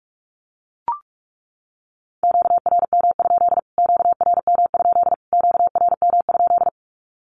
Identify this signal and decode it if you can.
Morse code, heard as a keyed tone, about 44 wpm, 700 Hz. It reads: QRM? QRM? QRM?
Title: Q-Codes Repeated 3x 45wpm